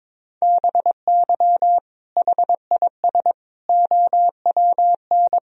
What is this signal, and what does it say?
Morse code, heard as a keyed tone, about 22 wpm, 700 Hz. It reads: BY HIS OWN
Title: Sentences from Top 1000 Words - Review Part 1 - 22wpm